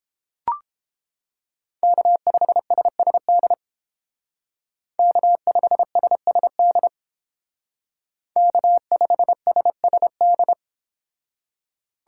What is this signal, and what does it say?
Morse code, heard as a keyed tone, about 30 wpm, 700 Hz. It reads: K5SSD K5SSD K5SSD